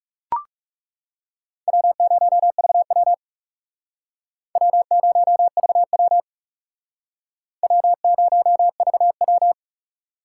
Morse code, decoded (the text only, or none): W0VW W0VW W0VW